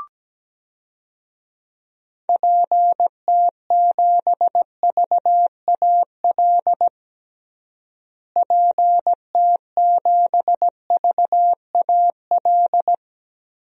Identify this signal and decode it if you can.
Morse code, heard as a keyed tone, about 17 wpm, 700 Hz. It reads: PT7VAL PT7VAL
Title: DXCC-100 Call Signs - 6 Characters - Repeated 3x - 17wpm